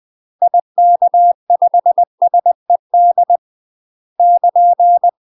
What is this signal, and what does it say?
Morse code, heard as a keyed tone, about 20 wpm, 700 Hz. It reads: IK5SED <KN>
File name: UK MASTER QSOs TRAINING / 20WPM / UK MASTER QSOs 0052